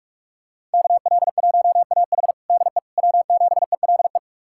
Morse code, heard as a keyed tone, about 42 wpm, 700 Hz. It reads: KL1AH DE W7ELE
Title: USA Callsign DE Callsign 45wpm